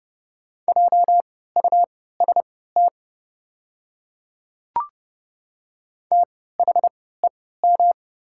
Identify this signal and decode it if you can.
Morse code, heard as a keyed tone, about 30 wpm, spaced as for 17 wpm, 690 Hz. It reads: JUST THEM
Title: 3x Character Spacing - Top 200 Words - 30wpm